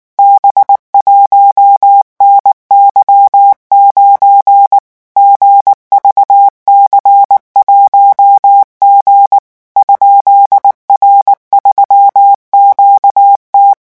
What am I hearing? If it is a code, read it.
Morse code, heard as a keyed tone, about 19 wpm, 800 Hz. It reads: B1NY9 GVC1G ?R3QT
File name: morse mp3 / revision12-vitesse 18 mots